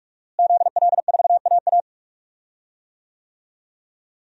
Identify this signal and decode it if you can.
Morse code, heard as a keyed tone, about 46 wpm, 700 Hz. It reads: ZL4AA
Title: DXCC-100 Call Signs - 5 Characters - Repeated 3x - 45wpm